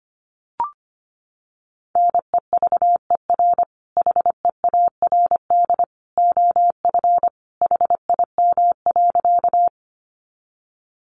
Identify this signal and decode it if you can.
Morse code, heard as a keyed tone, about 25 wpm, 700 Hz. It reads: NEVER HEARD OF HIM.
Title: Sentences from Top 400 Words 25wpm